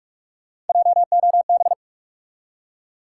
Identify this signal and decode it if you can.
Morse code, heard as a keyed tone, about 46 wpm, 700 Hz. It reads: JOB
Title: Three Letter Words - 45wpm